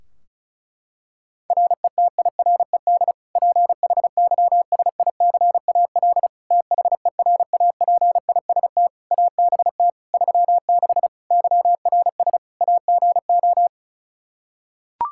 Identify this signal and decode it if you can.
Morse code, heard as a keyed tone, about 36 wpm, 700 Hz. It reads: RETIRED PHYSICAL THERAPIST ABT 36 YRS AGO
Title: Practice QSO - Exchange 3 - More Info - 35wpm